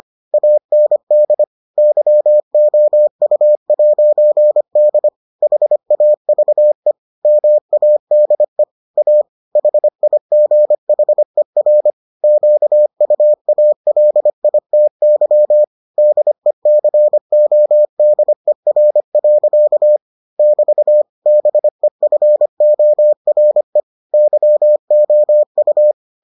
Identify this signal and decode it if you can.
Morse code, heard as a keyed tone, about 25 wpm, 600 Hz. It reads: AND YOU'D HAVE MADE A HIGHER QUALITY DECODER. <BT> BEFORE YOU